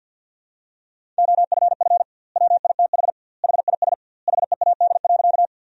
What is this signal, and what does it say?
Morse code, heard as a keyed tone, about 46 wpm, 700 Hz. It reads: OFF WITH HIS HEAD.
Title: 2x Word-Spacing - Sentences from Top 300 Words - 50wpm